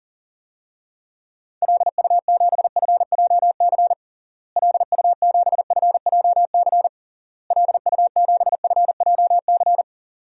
Morse code, heard as a keyed tone, about 40 wpm, 700 Hz. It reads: LU7FJC LU7FJC LU7FJC
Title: DXCC-100 Call Signs - 6 Characters - Repeated 3x - 40wpm